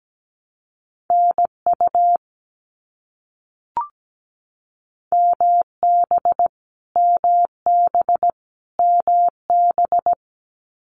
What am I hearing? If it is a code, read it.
Morse code, heard as a keyed tone, about 17 wpm, 700 Hz. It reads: NU MB MB MB
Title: Canadian Provinces and Territories - Repeated 3x - 17wpm